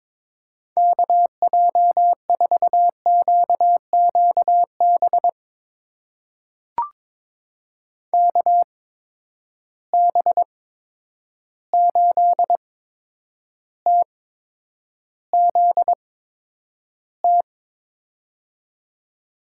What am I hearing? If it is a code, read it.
Morse code, heard as a keyed tone, about 21 wpm, spaced as for 9 wpm, 700 Hz. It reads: KJ4QQB KB8TZT
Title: USA Call Signs - 6 Characters - 8x Character Spacing - 22wpm